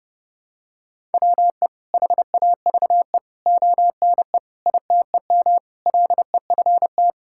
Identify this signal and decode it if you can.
Morse code, heard as a keyed tone, about 32 wpm, 700 Hz. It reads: WE HAVE ONE ITEM LEFT
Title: Sentences from Top 1000 Words - Review Part 6 - 30wpm